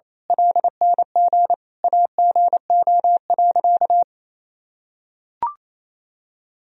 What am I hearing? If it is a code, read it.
Morse code, heard as a keyed tone, about 28 wpm, 700 Hz. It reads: LNG AGO.
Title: Sentences from Top 500 Words and Abbreviations 28wpm